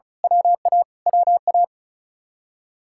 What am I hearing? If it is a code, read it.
Morse code, heard as a keyed tone, about 33 wpm, 700 Hz. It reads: WA WA
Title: Speed-Racing Common QSO Phrases - 35wpm